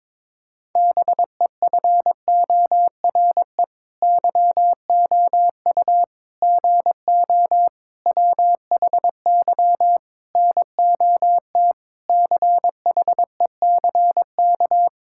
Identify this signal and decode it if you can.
Morse code, heard as a keyed tone, about 22 wpm, 700 Hz. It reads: BEFORE YOU GO WHY NOT CHECK
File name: Sets of 9 Words - 22wpm